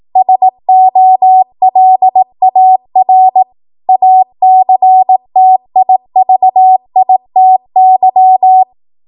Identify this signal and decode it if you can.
Morse code, heard as a keyed tone, about 18 wpm, 750 Hz. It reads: SOLAR ACTIVITY
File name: Morse / 18WPM